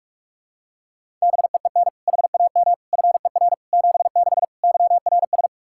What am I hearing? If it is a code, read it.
Morse code, heard as a keyed tone, about 45 wpm, 700 Hz. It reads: BEEN HAM FER 76 YRS
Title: K1USN SST QSO Practice 45wpm